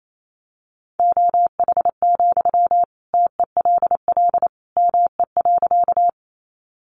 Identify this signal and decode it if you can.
Morse code, heard as a keyed tone, about 28 wpm, 700 Hz. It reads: OH, TELL ME.
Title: Sentences from Top 500 Words 28wpm